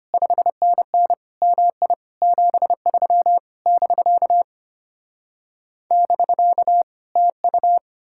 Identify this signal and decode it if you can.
Morse code, heard as a keyed tone, about 33 wpm, 700 Hz. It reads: 5NN MI 73 <BK> <BK> TU